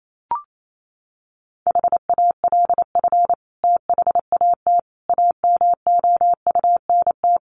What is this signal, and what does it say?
Morse code, heard as a keyed tone, about 28 wpm, 700 Hz. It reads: HALF THAT AMOUNT
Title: Sentences from Top 1400 Words 28wpm